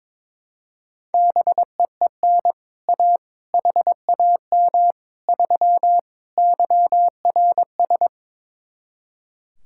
Morse code, heard as a keyed tone, about 22 wpm, 700 Hz. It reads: BEEN A HAM 3 YRS
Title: Speed-Racing Common QSO Phrases Encore - 22wpm